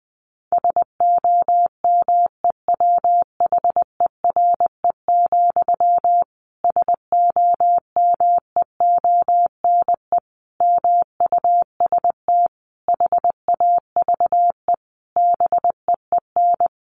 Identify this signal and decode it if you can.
Morse code, heard as a keyed tone, about 20 wpm, 700 Hz. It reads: SOMEWHERE, SOMEONE MUST HAVE BEEN
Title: A Sound Of Thunder 20wpm